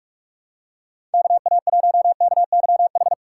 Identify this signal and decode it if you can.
Morse code, heard as a keyed tone, about 43 wpm, 700 Hz. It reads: KA1KYH
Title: USA Call Signs - 6 Characters - 3x Character Spacing - 45wpm